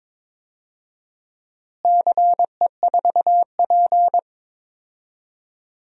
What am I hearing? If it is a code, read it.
Morse code, heard as a keyed tone, about 22 wpm, 700 Hz. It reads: CE4P
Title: DXCC-100 Call Signs - 4 Characters - Repeated 3x - 22wpm